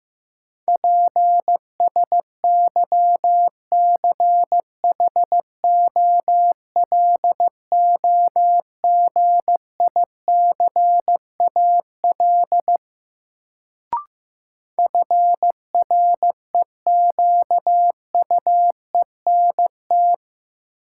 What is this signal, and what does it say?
Morse code, heard as a keyed tone, about 15 wpm, 700 Hz. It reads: PSYCHOLOGICAL FREQUENT
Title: Sentences from Top 1400 Words - New Word Drill Alt - 15wpm